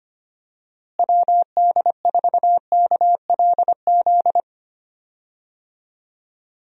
Morse code, heard as a keyed tone, about 25 wpm, 700 Hz. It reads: WD4KLZ
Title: USA Call Signs - 6 Characters - Repeated 3x - 25wpm